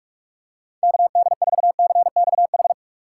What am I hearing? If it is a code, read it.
Morse code, heard as a keyed tone, about 46 wpm, 700 Hz. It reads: KD4CXH
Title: USA Call Signs - 6 Characters - 4x Character Spacing - 45wpm